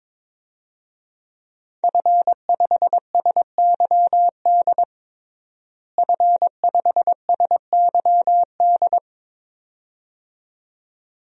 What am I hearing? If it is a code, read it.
Morse code, heard as a keyed tone, about 22 wpm, 700 Hz. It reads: F5SYD F5SYD
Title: DXCC-100 Call Signs - 5 Characters - Repeated 2x - 22wpm